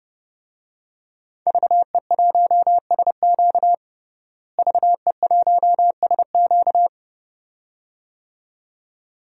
Morse code, heard as a keyed tone, about 30 wpm, 700 Hz. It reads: VE1SQ VE1SQ
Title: DXCC-100 Call Signs - 5 Characters - Repeated 2x - 30wpm